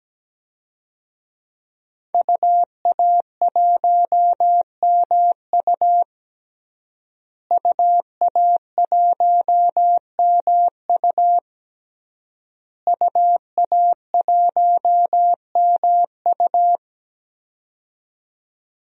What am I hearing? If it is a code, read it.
Morse code, heard as a keyed tone, about 17 wpm, 700 Hz. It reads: UA1MU UA1MU UA1MU
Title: DXCC-100 Call Signs - 5 Characters - Repeated 3x - 17wpm